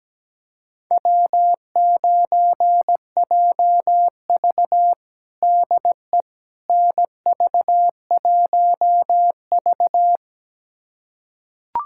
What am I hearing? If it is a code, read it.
Morse code, heard as a keyed tone, about 17 wpm, 700 Hz. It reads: W9JV DE NV1V
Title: USA Callsign DE Callsign - 4 Characters - 17wpm